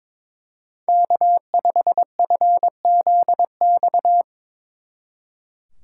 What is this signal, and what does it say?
Morse code, heard as a keyed tone, about 24 wpm, 700 Hz. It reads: K5FZX
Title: Speed Racing - USA 1x3 Call Signs 22wpm